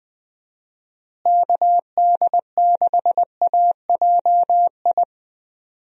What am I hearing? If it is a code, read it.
Morse code, heard as a keyed tone, about 21 wpm, 700 Hz. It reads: KD6AJI
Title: USA Call Signs - 6 Characters - Repeated 2x - 20wpm